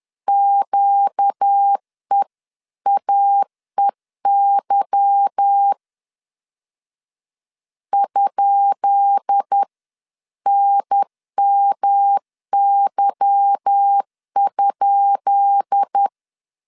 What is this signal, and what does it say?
Morse code, heard as a keyed tone, about 10 wpm, 800 Hz. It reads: QEAEY ?NMY?
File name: LEZIONI / 10 SIGNS / 10B INT 10wpm 2844